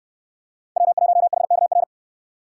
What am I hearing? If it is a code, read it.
Morse code, heard as a keyed tone, about 69 wpm, 700 Hz. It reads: W1HKR